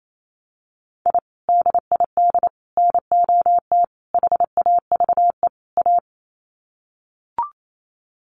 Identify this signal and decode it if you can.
Morse code, heard as a keyed tone, about 28 wpm, 700 Hz. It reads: I DID NOT HAVE A
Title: Sets of 5 Words 28wpm